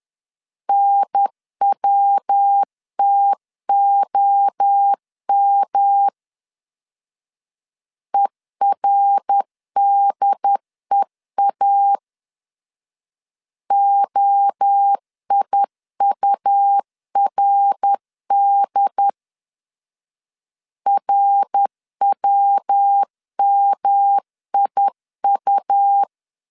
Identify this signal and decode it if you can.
Morse code, heard as a keyed tone, about 10 wpm, 800 Hz. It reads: NWTOM ERDEA OIURD RWMIU
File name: CW / DET / 1 3 1500C S C 10wpm 3335